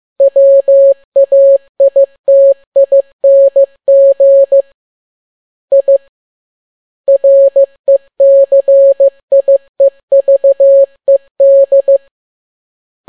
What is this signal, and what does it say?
Morse code, heard as a keyed tone, about 15 wpm, 550 Hz. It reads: WAITING I RECIEVED
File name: n6tt p1 15C13